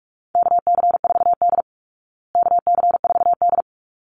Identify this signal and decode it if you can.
Morse code, heard as a keyed tone, about 44 wpm, 700 Hz. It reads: KC4D KC4D
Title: Call Signs Repeated 3x - 45wpm